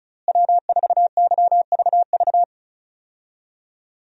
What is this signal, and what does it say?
Morse code, heard as a keyed tone, about 36 wpm, 700 Hz. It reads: W4YVV
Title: USA Call Signs - 5 Characters - Repeated 2x - 35wpm